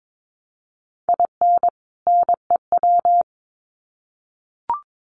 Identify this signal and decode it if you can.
Morse code, heard as a keyed tone, about 22 wpm, 700 Hz. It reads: IN NEW
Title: Sets of 2 Words 22wpm